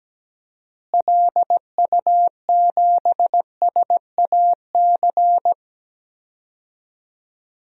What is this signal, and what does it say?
Morse code, heard as a keyed tone, about 17 wpm, 700 Hz. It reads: LU7SAC
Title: DXCC-100 Call Signs - 6 Characters - Repeated 3x - 17wpm